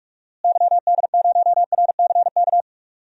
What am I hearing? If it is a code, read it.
Morse code, heard as a keyed tone, about 46 wpm, 700 Hz. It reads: YD0RCK
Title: DXCC-100 Call Signs - 6 Characters - Repeated 3x - 45wpm